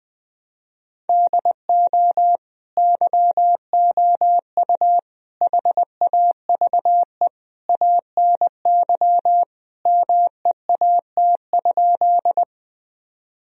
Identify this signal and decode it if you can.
Morse code, heard as a keyed tone, about 20 wpm, 700 Hz. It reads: DO YOU HAVE ANY MEAT?
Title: Sentences from Top 1000 Words - Review Part 3 - 20wpm